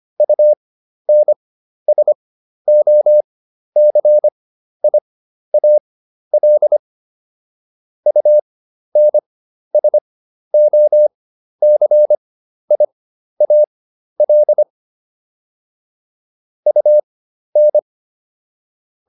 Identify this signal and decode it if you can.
Morse code, heard as a keyed tone, about 25 wpm, spaced as for 12 wpm, 600 Hz. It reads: UNSOCIAL UNSOCIAL UN